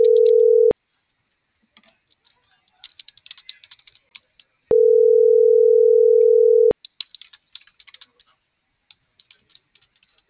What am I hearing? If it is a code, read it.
Ringback tone (North America), 440 + 480 Hz.